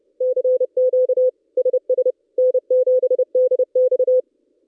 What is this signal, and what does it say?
Morse code, heard as a keyed tone, about 31 wpm, 510 Hz. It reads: CQ SS N7DX
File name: contests / ss-cw-k5tr-op-n5rz -1700